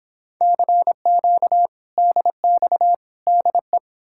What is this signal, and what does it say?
Morse code, heard as a keyed tone, about 31 wpm, 700 Hz. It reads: CQ DX DE